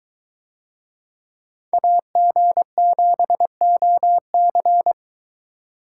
Morse code, heard as a keyed tone, about 22 wpm, 700 Hz. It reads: AG7OC